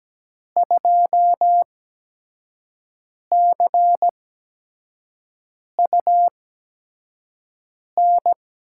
Morse code, heard as a keyed tone, about 17 wpm, spaced as for 5 wpm, 700 Hz. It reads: 2CUN
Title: USA Call Signs - 5 Characters - 8x Character Spacing - 17wpm